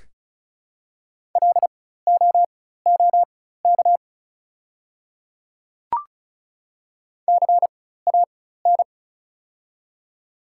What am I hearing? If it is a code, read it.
Morse code, heard as a keyed tone, about 36 wpm, spaced as for 16 wpm, 700 Hz. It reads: LOOK CAN
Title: 4x Character Spacing - Top 200 Words - 35wpm